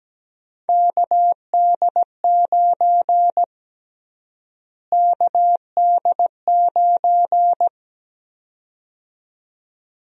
Morse code, heard as a keyed tone, about 17 wpm, 700 Hz. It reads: KD9 KD9
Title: USA Call Sign Prefixes - Repeated 2x - 17wpm